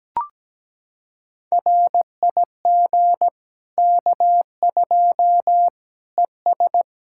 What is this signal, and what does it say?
Morse code, heard as a keyed tone, about 17 wpm, 700 Hz. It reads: RIG K2 ES
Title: Practice QSO - Exchange 2 - Info - 17wpm